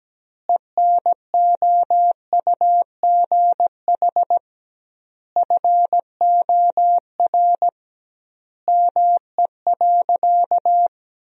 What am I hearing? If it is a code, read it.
Morse code, heard as a keyed tone, about 17 wpm, 700 Hz. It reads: ENOUGH FOR ME.
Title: 2x Word-Spacing - Sentences from Top 400 Words - 17wpm